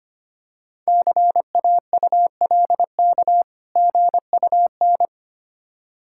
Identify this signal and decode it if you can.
Morse code, heard as a keyed tone, about 25 wpm, 700 Hz. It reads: CAULK GUN
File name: practice / 25wpm / Difficult Words with Hints - 25wpm